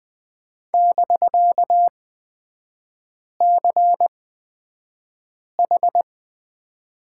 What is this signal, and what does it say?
Morse code, heard as a keyed tone, about 20 wpm, spaced as for 5 wpm, 700 Hz. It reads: <BK>CH
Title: 250 Lesson 40e - BK - Three Characters 20wpm